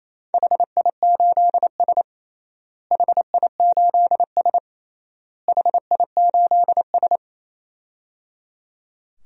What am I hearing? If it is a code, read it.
Morse code, heard as a keyed tone, about 28 wpm, 700 Hz. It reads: HI8S HI8S HI8S
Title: DXCC-100 Call Signs - 4 Characters - Repeated 3x - 28wpm